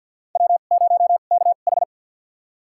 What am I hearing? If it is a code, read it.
Morse code, heard as a keyed tone, about 52 wpm, spaced as for 37 wpm, 700 Hz. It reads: W0KH
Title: USA Call Signs - 4 Characters - 2x Character Spacing - 50wpm